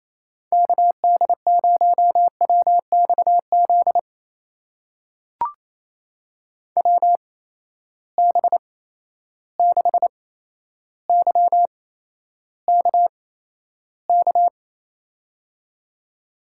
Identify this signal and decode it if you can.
Morse code, heard as a keyed tone, about 28 wpm, spaced as for 12 wpm, 700 Hz. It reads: KD0WXZ WB6YKK
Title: USA Call Signs - 6 Characters - 8x Character Spacing - 28wpm